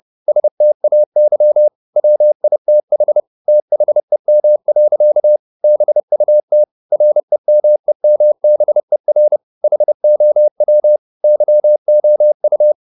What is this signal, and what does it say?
Morse code, heard as a keyed tone, about 30 wpm, 600 Hz. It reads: STAY WITH THEM. BUT REMEMBER HOW YOU
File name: the old man and the sea / the old man and the sea 1 30wpm